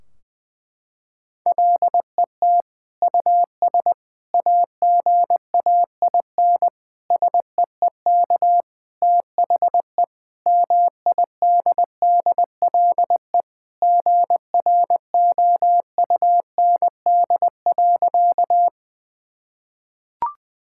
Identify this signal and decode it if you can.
Morse code, heard as a keyed tone, about 20 wpm, 700 Hz. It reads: LET US AGAIN SEEK THE MIDDLE GROUND.
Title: Sentences from Top 1600 Words 20wpm